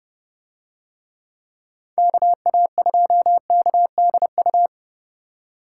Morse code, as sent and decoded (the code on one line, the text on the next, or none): -.- .- ..--- -.- -.. ..-
KA2KDU